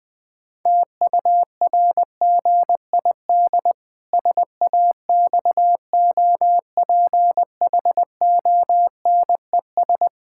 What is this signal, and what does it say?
Morse code, heard as a keyed tone, about 20 wpm, 700 Hz. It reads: TURGID SAXOPHONES
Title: Pangram Challenge v2 - 20wpm